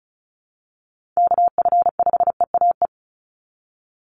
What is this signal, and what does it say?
Morse code, heard as a keyed tone, about 34 wpm, 700 Hz. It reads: KF5EAE
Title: USA Call Signs 35wpm